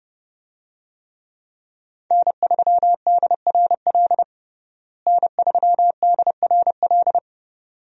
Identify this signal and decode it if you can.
Morse code, heard as a keyed tone, about 30 wpm, 700 Hz. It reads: N3DRL N3DRL
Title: USA Call Signs - 5 Characters - Repeated 2x - 30wpm